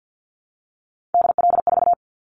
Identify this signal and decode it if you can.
Morse code, heard as a keyed tone, about 50 wpm, 700 Hz. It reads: DL4
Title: Callsign Prefixes for Top 50 DXCC - 50wpm